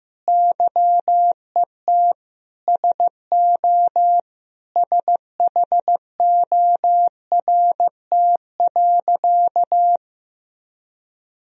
Morse code, heard as a keyed tone, about 16 wpm, 700 Hz. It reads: YET SO SHORT.
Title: Sentences from Top 500 Words and Abbreviations 15wpm